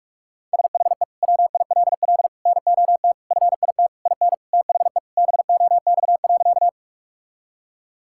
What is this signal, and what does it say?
Morse code, heard as a keyed tone, about 41 wpm, 700 Hz. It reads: SHE WILL NOT FIT IN THE BOX.
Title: Sentences from Top 1000 Words - Review Part 5 - 45wpm